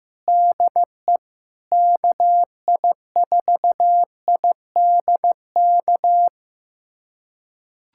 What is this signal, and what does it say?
Morse code, heard as a keyed tone, about 15 wpm, 700 Hz. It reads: DE KI4IDK
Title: USA Callsign DE Callsign - 6 Characters - 15wpm